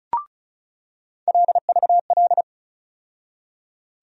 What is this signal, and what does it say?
Morse code, heard as a keyed tone, about 34 wpm, 700 Hz. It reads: LVL
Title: USA Call Sign Postfixes - 3 Letters 35wpm